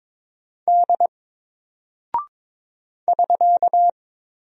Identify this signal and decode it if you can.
Morse code, heard as a keyed tone, about 22 wpm, 700 Hz. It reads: D <SK>
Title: Single Character (Letter:Number:Punctuation:Prosign) 22wpm